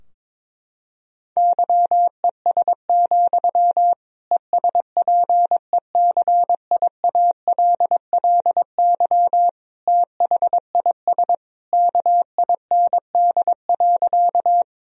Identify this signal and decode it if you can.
Morse code, heard as a keyed tone, about 22 wpm, 700 Hz. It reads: YES, ESPECIALLY THIS KIND.
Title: Sentences from Top 1000 Words - Review Part 4 - 22wpm